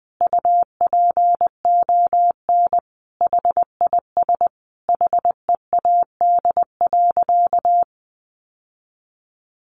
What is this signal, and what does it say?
Morse code, heard as a keyed tone, about 20 wpm, 700 Hz. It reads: UPON HIS HEAD.
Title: Sentences from Top 1100 Words 20wpm